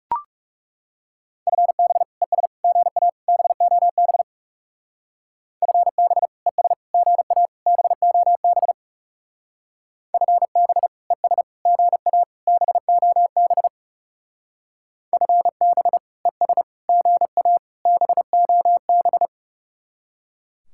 Morse code, heard as a keyed tone, about 36 wpm, 700 Hz. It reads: FB ES GA BOB FB ES GA BOB FB ES GA BOB FB ES GA BOB
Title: Speed-Racing Common QSO Phrases - 30wpm